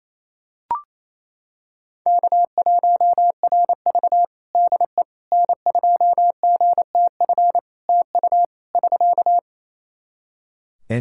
Morse code, heard as a keyed tone, about 28 wpm, 700 Hz. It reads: K1RV DE N2GTF TU <SK>
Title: Practice QSO - Exchange 4 - Ending - 28wpm